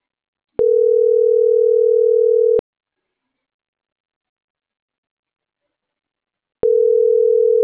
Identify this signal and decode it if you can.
Ringback tone (North America), 440 + 480 Hz.